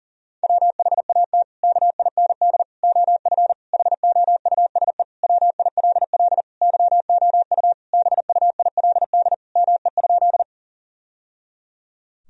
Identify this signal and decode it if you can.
Morse code, heard as a keyed tone, about 40 wpm, 700 Hz. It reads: WHAT KIND OF HOUSE WILL YOU BUILD ME?
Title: Sentences from Top 1000 Words - Review Part 1 - 40wpm